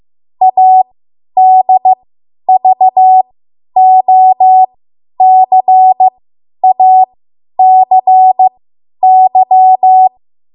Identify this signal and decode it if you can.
Morse code, heard as a keyed tone, about 15 wpm, spaced as for 10 wpm, 750 Hz. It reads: ADVOCACY
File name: Morse / 10WPM